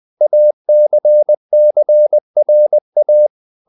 Morse code, heard as a keyed tone, about 20 wpm, 600 Hz. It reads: ACCRA
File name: CW / Practice / NationalCapitals20